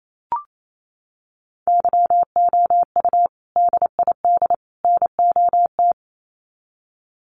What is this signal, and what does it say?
Morse code, heard as a keyed tone, about 28 wpm, 700 Hz. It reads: YOU DID NOT
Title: Sets of 3 Words 28wpm